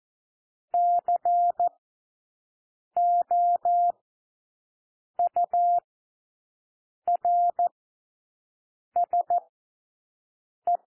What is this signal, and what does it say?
Morse code, heard as a keyed tone, about 14 wpm, spaced as for 6 wpm, 700 Hz. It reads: COURSE
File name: cwmp3 / W1 SOUND WORDS 14 wpm 0